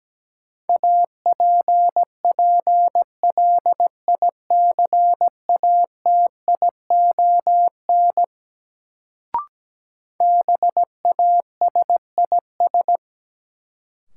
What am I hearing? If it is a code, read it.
Morse code, heard as a keyed tone, about 17 wpm, 700 Hz. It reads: APPLICATION BASIS
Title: Sentences from Top 1200 Words - New Word Drill Alt - 17wpm